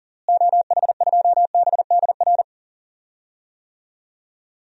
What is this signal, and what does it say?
Morse code, heard as a keyed tone, about 40 wpm, 700 Hz. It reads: OH2BDR
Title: DXCC-100 Call Signs - 6 Characters - Repeated 2x - 40wpm